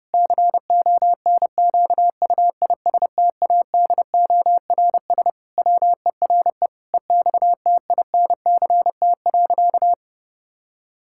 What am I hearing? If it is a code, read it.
Morse code, heard as a keyed tone, about 28 wpm, 700 Hz. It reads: CONQUISTADORS WERE EXTINCT.
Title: Pangram Challenge v2 - 30wpm